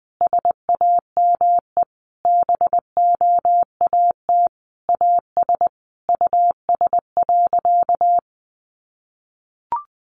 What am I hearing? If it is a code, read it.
Morse code, heard as a keyed tone, about 20 wpm, 700 Hz. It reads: SAME BOAT AS US.
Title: Sentences from Top 500 Words 20wpm